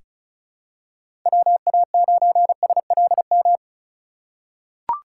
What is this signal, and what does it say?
Morse code, heard as a keyed tone, about 36 wpm, 700 Hz. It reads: WA9SLM
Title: USA Call Signs - 3x Character Spacing - 35wpm